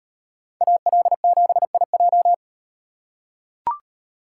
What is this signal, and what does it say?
Morse code, heard as a keyed tone, about 38 wpm, 700 Hz. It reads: AL7IJ